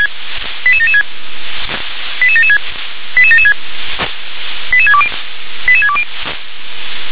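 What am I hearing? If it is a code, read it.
Five-tone selective call (ZVEI), heard as quick runs of five tones, tones of about 70 ms. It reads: 88776 88776 88776 88620 88620
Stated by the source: Five-tone ZVEI selective call 88775 received by SDR.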